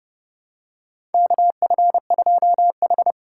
Morse code, heard as a keyed tone, about 30 wpm, 700 Hz. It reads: KF2H
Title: USA Call Signs - 4 Characters - 8x Character Spacing - 30wpm